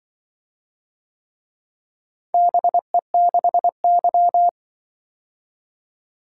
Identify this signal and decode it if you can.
Morse code, heard as a keyed tone, about 24 wpm, 700 Hz. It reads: BE6Y
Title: Speed-Racing - Four Letter-Number Combinations - 20wpm